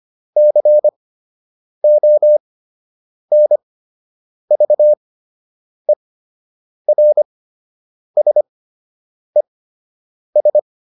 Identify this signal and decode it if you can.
Morse code, heard as a keyed tone, about 25 wpm, spaced as for 8 wpm, 600 Hz. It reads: CONVERSES